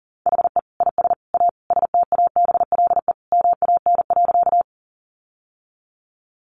Morse code, heard as a keyed tone, about 40 wpm, 700 Hz. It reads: HE IS A STABLE MAN.
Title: Sentences from Top 1500 Words 40wpm